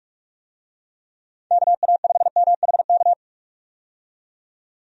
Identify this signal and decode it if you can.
Morse code, heard as a keyed tone, about 46 wpm, 700 Hz. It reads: KA5MHK
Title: USA Call Signs - 6 Characters - 45wpm